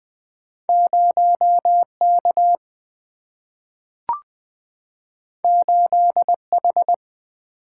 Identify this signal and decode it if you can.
Morse code, heard as a keyed tone, about 20 wpm, 700 Hz. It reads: 0 K 8 H
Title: Single-Letter Single-Number 20wpm